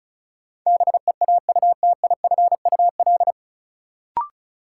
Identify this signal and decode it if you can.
Morse code, heard as a keyed tone, about 34 wpm, 700 Hz. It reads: BEAUTIFUL
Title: Sentences from Top 900 Words - New Word Drill - 35wpm